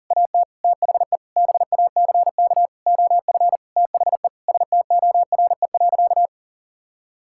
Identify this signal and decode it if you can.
Morse code, heard as a keyed tone, about 40 wpm, 700 Hz. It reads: AT THE BACK OF THE STORE.
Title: Sentences from Top 1000 Words - Review Part 5 - 40wpm